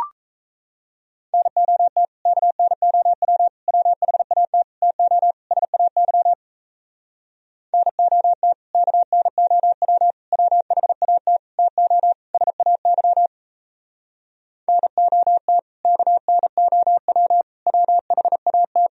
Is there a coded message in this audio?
Yes: Morse code from 1.3 to 18.9 s, heard as a keyed tone, about 38 wpm, 700 Hz. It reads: NOT KNOW WHAT TO SAY NOT KNOW WHAT TO SAY NOT KNOW WHAT